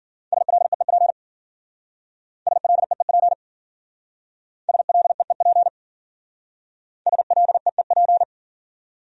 Morse code, heard as a keyed tone, about 52 wpm, 700 Hz. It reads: SLEEP SLEEP SLEEP SLEEP